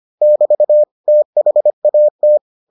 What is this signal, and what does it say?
Morse code, heard as a keyed tone, about 25 wpm, 600 Hz. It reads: <BT> THAT